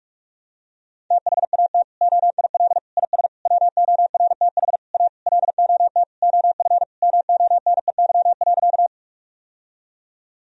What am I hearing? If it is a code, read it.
Morse code, heard as a keyed tone, about 41 wpm, 700 Hz. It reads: THAT OIL IS WORTH A LOT OF MONEY.
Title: Sentences from Top 1000 Words - Review Part 4 - 45wpm